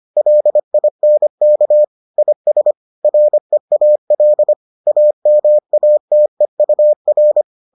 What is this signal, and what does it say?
Morse code, heard as a keyed tone, about 25 wpm, 600 Hz. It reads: LINK IS REAL AMATEUR